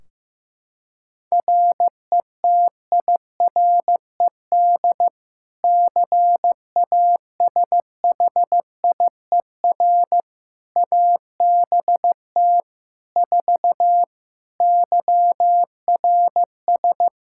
Morse code, heard as a keyed tone, about 16 wpm, 700 Hz. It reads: RETIRED CASHIER ABT 4 YRS
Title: Practice QSO - Exchange 3 - More Info - 15wpm